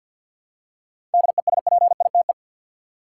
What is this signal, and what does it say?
Morse code, heard as a keyed tone, about 51 wpm, 700 Hz. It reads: DESPITE
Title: Seven Letter Words - 50wpm